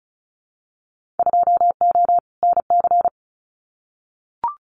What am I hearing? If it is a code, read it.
Morse code, heard as a keyed tone, about 36 wpm, 700 Hz. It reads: 2O NC
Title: Winter Field Day Exchanges 35wpm